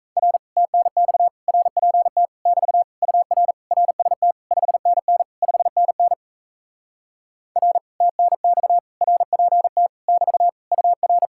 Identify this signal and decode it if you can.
Morse code, heard as a keyed tone, about 38 wpm, 700 Hz. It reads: R TNX RPT <BT> UR RST 5NN 5NN R TNX RPT <BT> UR